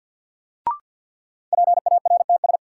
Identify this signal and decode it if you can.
Morse code, heard as a keyed tone, about 52 wpm, 700 Hz. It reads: PARTS
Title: Sets of 3 Words - New Word Drill - 50wpm